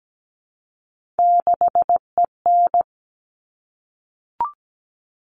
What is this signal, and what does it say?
Morse code, heard as a keyed tone, about 17 wpm, 700 Hz. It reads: 6EN
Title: All Cut Numbers - 3-Digit 17wpm